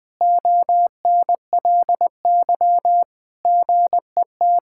Morse code, heard as a keyed tone, about 21 wpm, 700 Hz. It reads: ONLY GET
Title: Sentences from Top 1000 Words - Review Part 6 - 20wpm